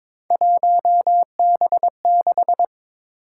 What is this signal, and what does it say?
Morse code, heard as a keyed tone, about 22 wpm, 700 Hz. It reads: 1B6
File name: practice / 22wpm / Three Letter-Number Combinations - 22wpm